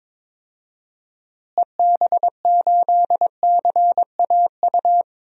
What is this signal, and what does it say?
Morse code, heard as a keyed tone, about 22 wpm, 700 Hz. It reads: EB8CAU
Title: DXCC-100 Call Signs - 6 Characters - Repeated 3x - 22wpm